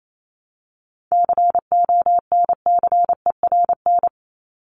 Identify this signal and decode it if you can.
Morse code, heard as a keyed tone, about 27 wpm, 700 Hz. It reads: CONCERN
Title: Top 751-1000 Words 28wpm